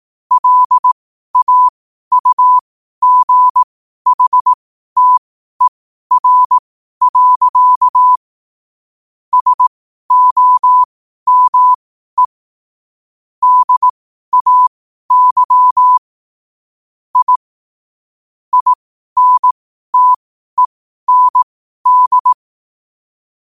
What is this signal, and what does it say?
Morse code, heard as a keyed tone, about 18 wpm, spaced as for 12 wpm, 1000 Hz. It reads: LAUGHTER. SOME DAY I INTEND
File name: qotdfiles / 12wpm